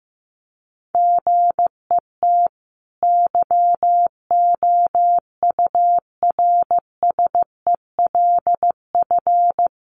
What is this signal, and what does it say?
Morse code, heard as a keyed tone, about 15 wpm, 700 Hz. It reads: GET YOURSELF
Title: Sentences from Top 1100 Words 15wpm